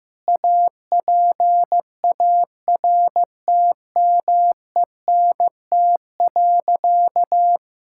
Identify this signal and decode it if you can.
Morse code, heard as a keyed tone, about 15 wpm, 700 Hz. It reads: APARTMENT.
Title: Sentences from Top 1000 Words - Review Part 6 - 15wpm